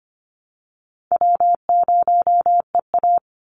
Morse code, heard as a keyed tone, about 25 wpm, 700 Hz. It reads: W0EA
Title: Call Signs Repeated 2x - 25wpm